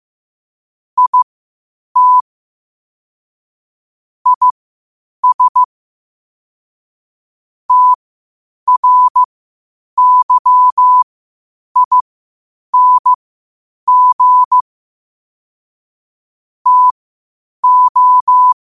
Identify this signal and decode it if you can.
Morse code, heard as a keyed tone, about 15 wpm, spaced as for 8 wpm, 1000 Hz. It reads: IT IS TRYING TO
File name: qotdfiles / 8wpm